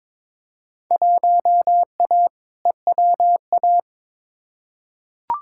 Morse code, heard as a keyed tone, about 22 wpm, 700 Hz. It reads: 1A EWA
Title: Field Day Exchanges Encore - 22wpm